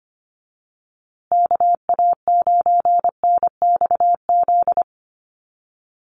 Morse code, heard as a keyed tone, about 25 wpm, 700 Hz. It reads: KA9NXZ
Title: USA Call Signs Repeated 2x 25wpm